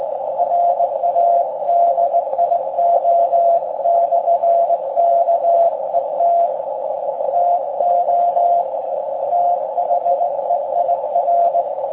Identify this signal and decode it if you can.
Morse code, heard as a keyed tone, about 19 wpm, 690 Hz, with radio noise. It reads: RADIO/KET TO THEIR